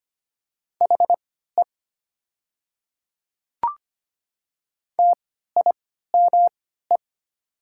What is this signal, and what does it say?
Morse code, heard as a keyed tone, about 25 wpm, spaced as for 14 wpm, 690 Hz. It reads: HE TIME
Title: 3x Character Spacing - Top 300 Words - 25wpm